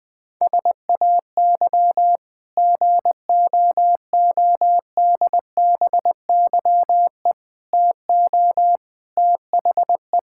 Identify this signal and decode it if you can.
Morse code, heard as a keyed tone, about 20 wpm, 700 Hz. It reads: SAY GOODBYE TO THE
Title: Sentences with 5-Word Phrases - Spoken Beginning - 20wpm